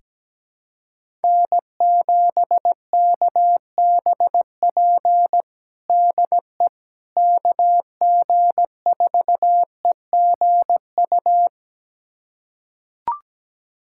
Morse code, heard as a keyed tone, about 17 wpm, 700 Hz. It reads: N7KBP DE KG4EGU
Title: USA Callsign DE Callsign 17wpm